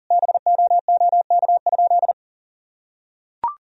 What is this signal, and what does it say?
Morse code, heard as a keyed tone, about 40 wpm, 700 Hz. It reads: BOOK?
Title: Sentences from Top 1000 Words - Review Part 4 - 40wpm